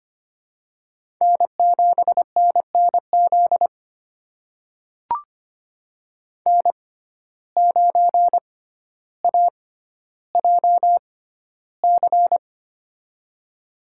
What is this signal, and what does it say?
Morse code, heard as a keyed tone, about 25 wpm, spaced as for 13 wpm, 700 Hz. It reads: N7NNZ N9AJC
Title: USA Call Signs - 5 Characters - 6x Character Spacing - 25wpm